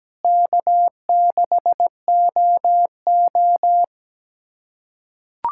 Morse code, heard as a keyed tone, about 17 wpm, 700 Hz. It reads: K6OO
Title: USA Callsign DE Callsign - 4 Characters - 17wpm